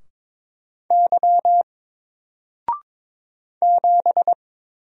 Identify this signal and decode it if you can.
Morse code, heard as a keyed tone, about 22 wpm, 700 Hz. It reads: Y 7
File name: Single-Letter-Number-22wpm